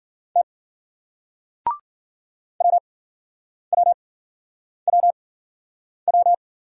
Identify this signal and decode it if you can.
Morse code, heard as a keyed tone, about 54 wpm, 700 Hz. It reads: T W W W W